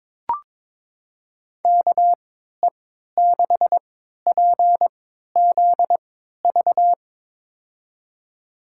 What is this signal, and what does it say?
Morse code, heard as a keyed tone, about 22 wpm, spaced as for 12 wpm, 700 Hz. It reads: KE6PZV
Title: USA Call Signs - 3x Character Spacing - 22wpm